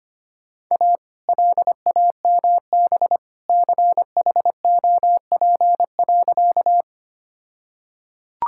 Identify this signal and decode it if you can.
Morse code, heard as a keyed tone, about 25 wpm, 700 Hz. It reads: A LAMB CHOP.
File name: practice / 25wpm / All-Age Appropriate Jokes - 25wpm